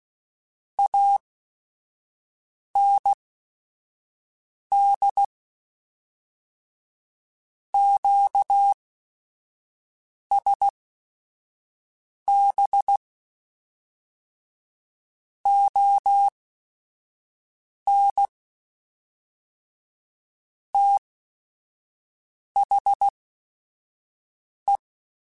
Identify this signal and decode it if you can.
Morse code, heard as a keyed tone, about 16 wpm, spaced as for 5 wpm, 800 Hz. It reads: AND QSB ON THE
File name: QSOs / QSOs-mp3s / WD0BGP 1@5